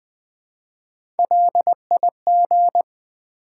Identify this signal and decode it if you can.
Morse code, heard as a keyed tone, about 20 wpm, 700 Hz. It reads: LIG
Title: Build A Word - 1 Letter At A Time - 5-Letter Words - 20wpm